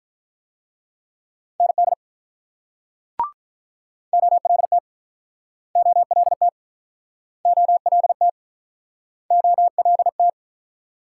Morse code, heard as a keyed tone, about 47 wpm, 700 Hz. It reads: ND OLT OLT OLT OLT